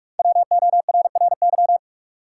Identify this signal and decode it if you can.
Morse code, heard as a keyed tone, about 47 wpm, 700 Hz. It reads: WORRY
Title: Five Letter Words - 45wpm